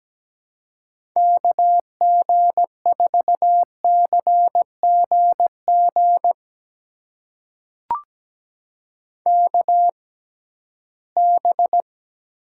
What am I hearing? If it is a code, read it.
Morse code, heard as a keyed tone, about 17 wpm, spaced as for 13 wpm, 700 Hz. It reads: KG4CGG KB
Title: USA Call Signs - 6x Character Spacing - 17wpm